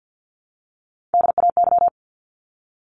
Morse code, heard as a keyed tone, about 50 wpm, 700 Hz. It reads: DAY
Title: Top 200 Words 50wpm